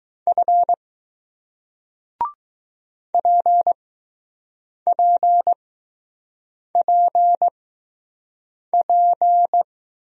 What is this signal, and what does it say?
Morse code, heard as a keyed tone, about 20 wpm, spaced as for 6 wpm, 700 Hz. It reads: F PPPP